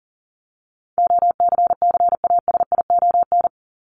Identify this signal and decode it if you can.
Morse code, heard as a keyed tone, about 40 wpm, 700 Hz. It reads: OCCASION
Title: Top 1501-1750 Words 40wpm